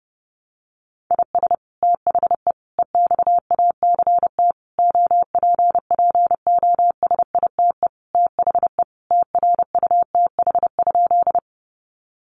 Morse code, heard as a keyed tone, about 30 wpm, 700 Hz. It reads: IS THE EXACT OPPOSITE THE TRUTH?
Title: Sentences from Top 1300 Words 30wpm